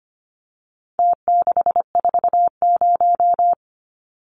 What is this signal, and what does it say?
Morse code, heard as a keyed tone, about 25 wpm, 700 Hz. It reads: T640
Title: Numbers - Contest Style - 4-Digit 25wpm